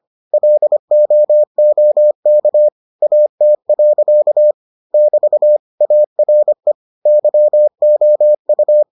Morse code, heard as a keyed tone, about 25 wpm, 600 Hz. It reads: LOOK AT. <BT> ARE YOU